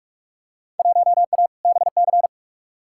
Morse code, heard as a keyed tone, about 43 wpm, 700 Hz. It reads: 1A BC
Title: Field Day Exchanges - 45wpm